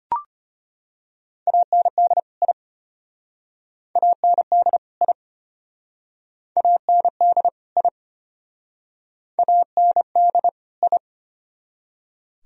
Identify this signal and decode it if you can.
Morse code, heard as a keyed tone, about 34 wpm, 700 Hz. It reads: AND I AND I AND I AND I